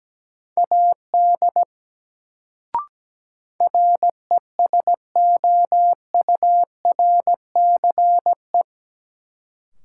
Morse code, heard as a keyed tone, about 17 wpm, 700 Hz. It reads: AD RESOURCE
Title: Sentences from Top 1300 Words - New Word Drill Alt - 17wpm